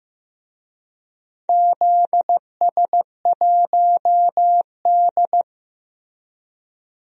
Morse code, heard as a keyed tone, about 15 wpm, 700 Hz. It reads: ZS1D
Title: DXCC-100 Call Signs - 4 Characters - Repeated 2x - 15wpm